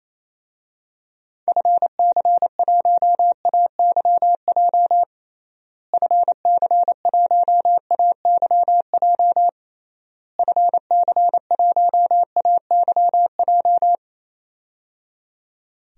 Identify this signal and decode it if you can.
Morse code, heard as a keyed tone, about 27 wpm, 700 Hz. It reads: FC1AYJ FC1AYJ FC1AYJ
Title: DXCC-100 Call Signs - 6 Characters - Repeated 3x - 28wpm